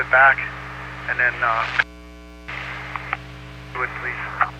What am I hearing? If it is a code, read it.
Emergency Alert System (SAME) tones, heard as harsh buzzing bursts, with speech. It is heard from 1.0 to 4.3 s.